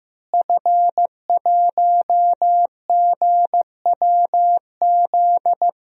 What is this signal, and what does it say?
Morse code, heard as a keyed tone, about 15 wpm, 700 Hz. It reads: F1GWZ
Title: DXCC-100 Call Signs - 5 Characters - Repeated 2x - 15wpm